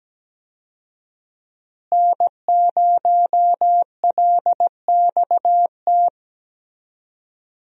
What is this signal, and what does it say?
Morse code, heard as a keyed tone, about 17 wpm, 700 Hz. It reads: N0LXT
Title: USA Call Signs - 5 Characters - Repeated 3x - 17wpm